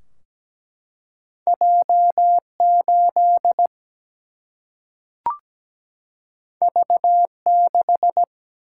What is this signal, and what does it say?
Morse code, heard as a keyed tone, about 17 wpm, 700 Hz. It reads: J8 V6
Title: Single-Letter Single-Number 17wpm